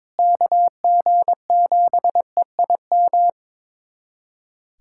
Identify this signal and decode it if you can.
Morse code, heard as a keyed tone, about 22 wpm, 700 Hz. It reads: KG7EIM
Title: USA Callsign DE Callsign 22wpm